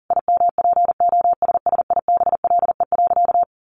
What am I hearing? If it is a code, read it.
Morse code, heard as a keyed tone, about 40 wpm, 700 Hz. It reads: IMPOSSIBLE.
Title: Sentences from Top 1100 Words 40wpm